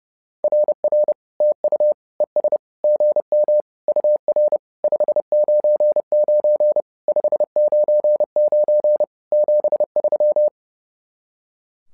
Morse code, heard as a keyed tone, about 30 wpm, 600 Hz. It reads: RR TU ES GM UR 599 599 73
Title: SOTA QSO Practice 30wpm